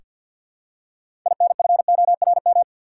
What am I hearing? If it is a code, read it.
Morse code, heard as a keyed tone, about 47 wpm, 700 Hz. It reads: INFORM
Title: Six Letter Words - 50wpm